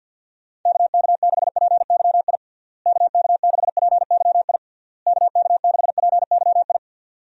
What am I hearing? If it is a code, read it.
Morse code, heard as a keyed tone, about 52 wpm, 700 Hz. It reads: KK6PYI KK6PYI KK6PYI
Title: USA Call Signs - 6 Characters - Repeated 3x - 50wpm